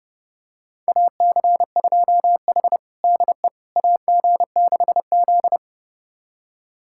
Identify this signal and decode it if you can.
Morse code, heard as a keyed tone, about 30 wpm, 700 Hz. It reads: AC2H DE AG6Z
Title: USA Callsign DE Callsign - 4 Characters - 30wpm